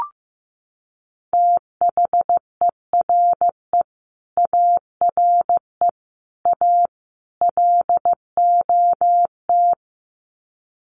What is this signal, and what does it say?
Morse code, heard as a keyed tone, about 15 wpm, 700 Hz. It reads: THERE ARE A LOT
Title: Sets of 4 Words 15wpm